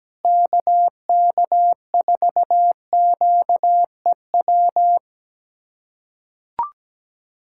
Morse code, heard as a keyed tone, about 17 wpm, 700 Hz. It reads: KK4QEW
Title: USA Callsign DE Callsign 17wpm